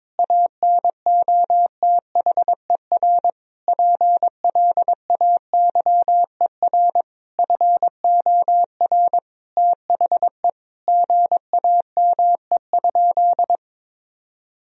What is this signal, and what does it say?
Morse code, heard as a keyed tone, about 22 wpm, 700 Hz. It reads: ANOTHER PLAYER FOR THE GAME?
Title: Sentences from Top 1000 Words - Review Part 4 - 22wpm